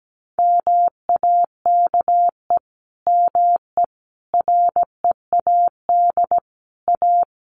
Morse code, heard as a keyed tone, about 17 wpm, 700 Hz. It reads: MAKE ME READ A
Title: Sentences from Top 1100 Words 17wpm